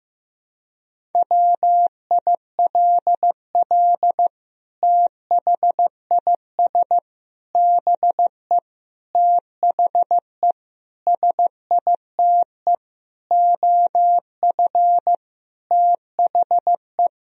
Morse code, heard as a keyed tone, about 15 wpm, 700 Hz. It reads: WILL THIS BE THE SITE OF THE
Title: Sentences from Top 1000 Words - Review Part 1 - 15wpm